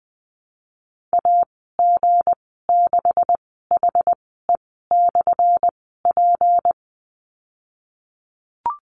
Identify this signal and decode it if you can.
Morse code, heard as a keyed tone, about 20 wpm, spaced as for 15 wpm, 700 Hz. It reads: AG6HE/P
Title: 179 Lesson 29g - Slash - Call Signs 20wpm